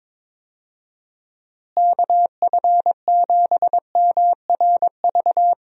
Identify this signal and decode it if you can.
Morse code, heard as a keyed tone, about 22 wpm, 700 Hz. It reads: KF7MRV
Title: USA Call Signs - 6 Characters - Repeated 2x - 22wpm